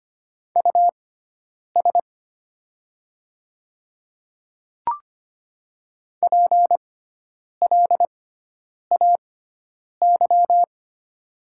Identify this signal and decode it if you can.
Morse code, heard as a keyed tone, about 25 wpm, spaced as for 9 wpm, 700 Hz. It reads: US PLAY
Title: 6x Character Spacing - Top 200 Words - 25wpm